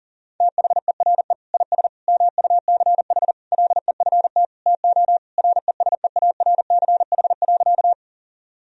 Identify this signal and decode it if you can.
Morse code, heard as a keyed tone, about 40 wpm, 700 Hz. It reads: THERE IS MUCH LEFT TO RESEARCH.
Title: Sentences from Top 1000 Words - Review Part 4 - 40wpm